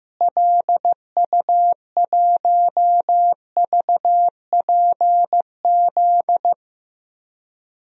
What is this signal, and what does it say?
Morse code, heard as a keyed tone, about 15 wpm, 700 Hz. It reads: LU1VPZ
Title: DXCC-100 Call Signs - 6 Characters - Repeated 2x - 15wpm